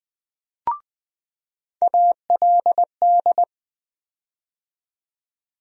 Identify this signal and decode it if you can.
Morse code, heard as a keyed tone, about 20 wpm, 700 Hz. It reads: ALD
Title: USA Call Sign Postfixes - 3 Letters 20wpm